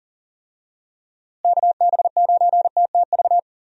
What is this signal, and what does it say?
Morse code, heard as a keyed tone, about 40 wpm, 700 Hz. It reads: KB9TTV
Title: USA Call Signs - 6 Characters - Repeated 3x - 40wpm